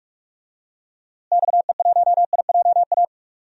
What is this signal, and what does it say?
Morse code, heard as a keyed tone, about 46 wpm, 700 Hz. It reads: XE1IJA
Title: DXCC-100 Call Signs - 6 Characters - Repeated 2x - 45wpm